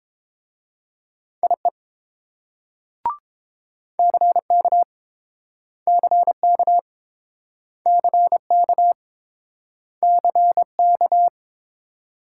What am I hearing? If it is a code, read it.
Morse code, heard as a keyed tone, about 32 wpm, 700 Hz. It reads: IE CK CK CK CK